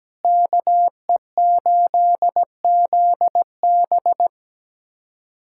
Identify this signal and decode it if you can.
Morse code, heard as a keyed tone, about 17 wpm, 700 Hz. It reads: KE8ZB
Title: USA Callsign DE Callsign - 5 Characters - 17wpm